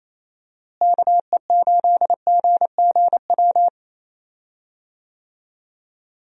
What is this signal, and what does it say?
Morse code, heard as a keyed tone, about 28 wpm, 700 Hz. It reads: KE8GGW
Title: USA Call Signs - 6 Characters - Repeated 3x - 28wpm